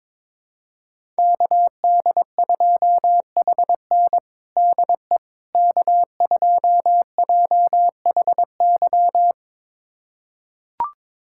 Morse code, heard as a keyed tone, about 21 wpm, 700 Hz. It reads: KD2HN DE K2JHY
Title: USA Callsign DE Callsign - 5 Characters - 22wpm